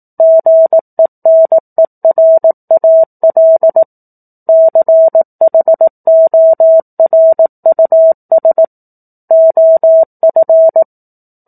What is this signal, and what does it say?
Morse code, heard as a keyed tone, about 18 wpm, 650 Hz. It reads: GENERAL CHORUS OF
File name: 04-Chapter-04 18 WPM